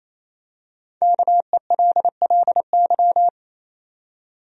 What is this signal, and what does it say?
Morse code, heard as a keyed tone, about 28 wpm, 700 Hz. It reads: KELLY
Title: USA Common Names - Max 5 Letters - 28wpm